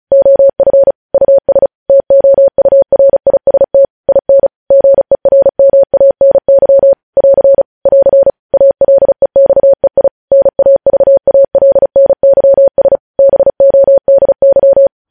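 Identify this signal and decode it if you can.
Morse code, heard as a keyed tone, about 36 wpm, 550 Hz. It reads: OF US TOURIST IN GERMANY <AR> <AR> ALEXEI NAVALNYS BODY
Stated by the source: Morse code news headlines sent at 35.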